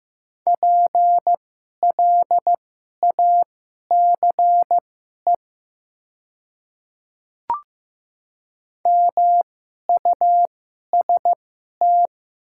Morse code, heard as a keyed tone, about 15 wpm, spaced as for 11 wpm, 700 Hz. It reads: PLACE MUST
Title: 2x Character Spacing - Top 200 Words - 15wpm